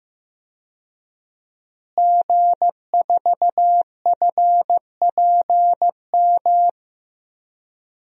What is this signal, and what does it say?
Morse code, heard as a keyed tone, about 15 wpm, 700 Hz. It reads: G4FPM
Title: DXCC-100 Call Signs - 5 Characters - Repeated 2x - 15wpm